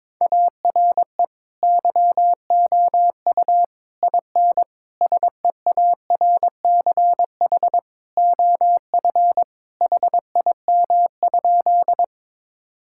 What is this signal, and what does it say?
Morse code, heard as a keyed tone, about 22 wpm, 700 Hz. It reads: ARE YOU IN SEARCH OF HIM?
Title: Sentences from Top 1000 Words - Review Part 5 - 22wpm